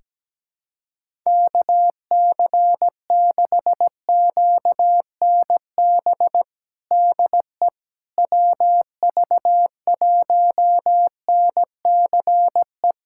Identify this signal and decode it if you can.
Morse code, heard as a keyed tone, about 17 wpm, 700 Hz. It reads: KC6QNB DE WV1NCE
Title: USA Callsign DE Callsign - 6 Characters - 17wpm